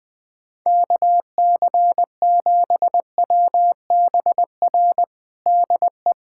Morse code, heard as a keyed tone, about 20 wpm, 700 Hz. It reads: KC7WBR DE
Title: USA Callsign DE Callsign - 6 Characters - 20wpm